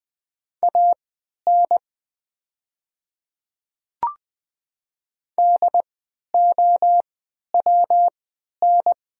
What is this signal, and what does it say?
Morse code, heard as a keyed tone, about 20 wpm, spaced as for 11 wpm, 700 Hz. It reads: AN DOWN
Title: 3x Character Spacing - Top 100 Words - 20wpm